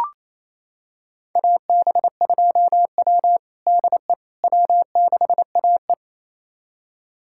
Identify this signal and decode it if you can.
Morse code, heard as a keyed tone, about 28 wpm, 700 Hz. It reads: AB2W DE W6AE
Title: USA Callsign DE Callsign - 4 Characters - 28wpm